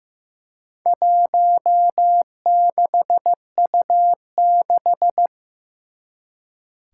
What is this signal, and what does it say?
Morse code, heard as a keyed tone, about 15 wpm, 700 Hz. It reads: 16U6
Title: Four Challenging Letter-Number Combinations 15wpm